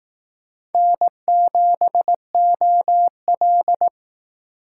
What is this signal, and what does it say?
Morse code, heard as a keyed tone, about 18 wpm, 700 Hz. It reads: N7OL
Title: Speed Racing - USA 1x2 Call Signs 15wpm